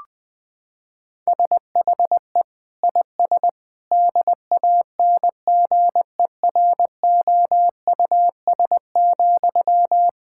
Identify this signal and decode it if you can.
Morse code, heard as a keyed tone, about 20 wpm, 700 Hz. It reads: SHE IS DANGEROUS,
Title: Sentences from Top 1600 Words 20wpm